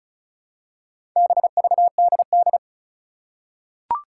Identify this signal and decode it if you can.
Morse code, heard as a keyed tone, about 36 wpm, 700 Hz. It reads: BVDD
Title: Four Challenging Letter-Number Combinations 35wpm